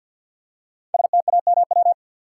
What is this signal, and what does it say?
Morse code, heard as a keyed tone, about 51 wpm, 700 Hz. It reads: STAMW
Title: Code Groups - 5 Letters - 50wpm